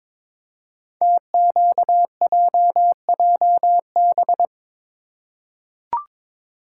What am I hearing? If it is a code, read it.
Morse code, heard as a keyed tone, about 22 wpm, 700 Hz. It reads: TQJJB
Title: Code Groups - 5 Letter:Numbers - 22wpm